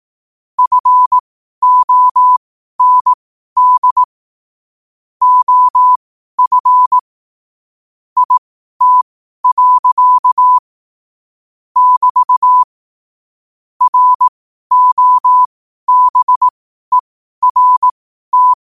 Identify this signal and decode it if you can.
Morse code, heard as a keyed tone, about 18 wpm, spaced as for 12 wpm, 1000 Hz. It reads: FOND OF IT. <BT> ROBERT
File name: qotdfiles / 12wpm